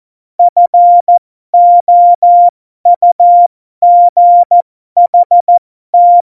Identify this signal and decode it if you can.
Morse code, heard as a keyed tone, about 14 wpm, spaced as for 12 wpm, 700 Hz. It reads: FOUGHT
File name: Morsekurs / Witze